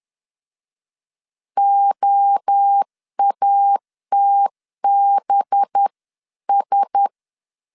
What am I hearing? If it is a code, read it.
Morse code, heard as a keyed tone, about 10 wpm, 800 Hz. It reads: OATBS
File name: CW / DET / 1 6 1500C S C 10wpm 4252